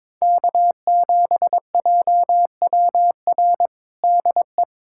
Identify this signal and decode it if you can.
Morse code, heard as a keyed tone, about 22 wpm, 700 Hz. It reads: K7JWR DE
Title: USA Callsign DE Callsign 22wpm